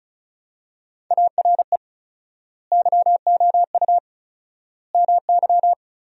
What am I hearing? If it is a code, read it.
Morse code, heard as a keyed tone, about 34 wpm, 700 Hz. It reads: ARE YOU MY
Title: 4x Word-Spacing - Sentences from Top 300 Words - 35wpm